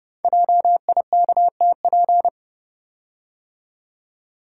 Morse code, heard as a keyed tone, about 28 wpm, 700 Hz. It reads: JIKTP
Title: Code Groups - 5 Letters - 30wpm